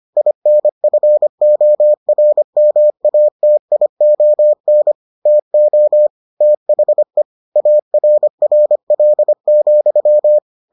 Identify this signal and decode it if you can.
Morse code, heard as a keyed tone, about 25 wpm, 600 Hz. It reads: INFORMATION TO THE ARRL,